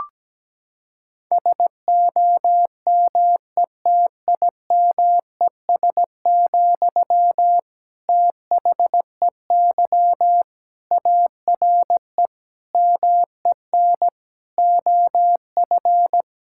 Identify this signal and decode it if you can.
Morse code, heard as a keyed tone, about 17 wpm, 700 Hz. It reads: SOMETIMES, THEY ARE MEN OF
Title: Sentences from Top 1000 Words - Review Part 5 - 17wpm